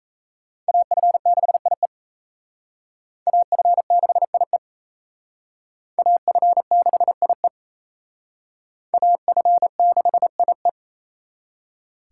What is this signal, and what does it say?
Morse code, heard as a keyed tone, about 34 wpm, 700 Hz. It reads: AF6IE AF6IE AF6IE AF6IE
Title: Speed Racing - USA 2x2 Call Signs 28wpm